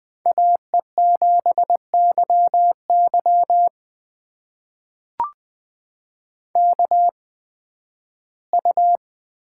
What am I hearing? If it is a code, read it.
Morse code, heard as a keyed tone, about 20 wpm, spaced as for 13 wpm, 700 Hz. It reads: AE7YY KU
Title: USA Call Signs - 8x Character Spacing - 20wpm